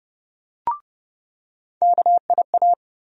Morse code, heard as a keyed tone, about 32 wpm, 700 Hz. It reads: KIA
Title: USA Call Sign Postfixes - 3 Letters 30wpm